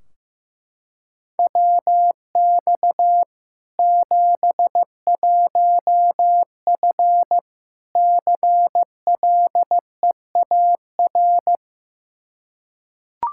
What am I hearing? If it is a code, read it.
Morse code, heard as a keyed tone, about 15 wpm, 700 Hz. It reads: WX 71F CLEAR
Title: K1USN SST QSO Practice 15wpm